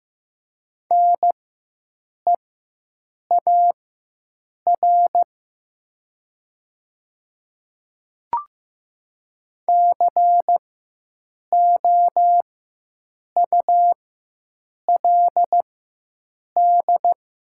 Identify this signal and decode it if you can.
Morse code, heard as a keyed tone, about 15 wpm, spaced as for 7 wpm, 700 Hz. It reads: NEAR COULD
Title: 4x Character Spacing - Top 200 Words - 15wpm